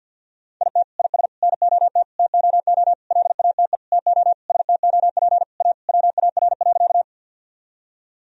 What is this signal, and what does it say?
Morse code, heard as a keyed tone, about 51 wpm, 700 Hz. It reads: IT IS NOT TOO LATE TO STOP A WAR.
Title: Sentences from Top 1000 Words - Review Part 2 - 50wpm